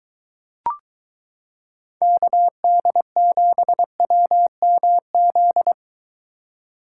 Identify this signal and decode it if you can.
Morse code, heard as a keyed tone, about 23 wpm, 700 Hz. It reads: KD7WMZ